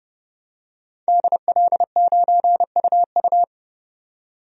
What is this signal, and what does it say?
Morse code, heard as a keyed tone, about 30 wpm, 700 Hz. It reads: DL9UU
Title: DXCC-100 Call Signs - 5 Characters - 30wpm